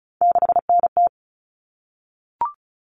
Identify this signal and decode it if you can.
Morse code, heard as a keyed tone, about 34 wpm, 700 Hz. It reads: 6NT
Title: All Cut Numbers - 3-Digit 35wpm